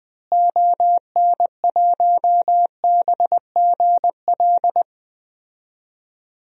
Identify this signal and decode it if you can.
Morse code, heard as a keyed tone, about 21 wpm, 700 Hz. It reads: ON1BGL
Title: DXCC-100 Call Signs - 6 Characters - Repeated 2x - 20wpm